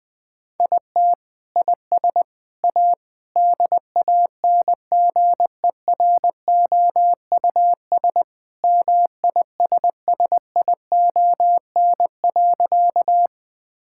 Morse code, heard as a keyed tone, about 20 wpm, 700 Hz. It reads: IT IS A DANGEROUS MISSION.
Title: Sentences from Top 1600 Words 20wpm